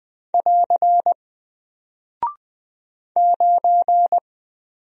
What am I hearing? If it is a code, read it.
Morse code, heard as a keyed tone, about 20 wpm, 700 Hz. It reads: <AR> 9
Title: Single Character (Letter:Number:Punctuation:Prosign) Alt 20wpm